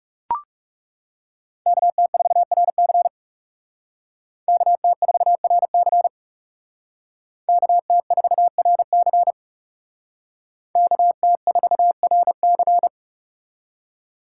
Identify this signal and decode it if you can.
Morse code, heard as a keyed tone, about 38 wpm, 700 Hz. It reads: KT4RC KT4RC KT4RC KT4RC